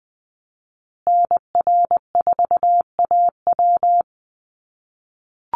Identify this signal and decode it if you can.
Morse code, heard as a keyed tone, about 20 wpm, 700 Hz. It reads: NR4AW
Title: USA Call Signs Repeated 3x 20wpm